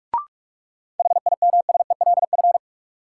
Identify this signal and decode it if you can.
Morse code, heard as a keyed tone, about 45 wpm, 700 Hz. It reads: HIMSELF
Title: Sentences from Top 1500 Words - New Word Drill Alt - 45wpm